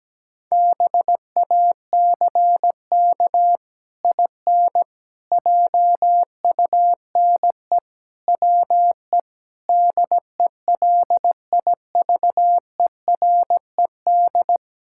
Morse code, heard as a keyed tone, about 17 wpm, 700 Hz. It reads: BACK IN JUNE WE DELIVERED
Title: Pangram Challenge v2 - 17wpm